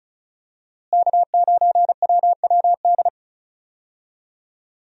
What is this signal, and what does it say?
Morse code, heard as a keyed tone, about 35 wpm, 700 Hz. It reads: K9WWD
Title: USA Call Signs - 5 Characters - Repeated 3x - 35wpm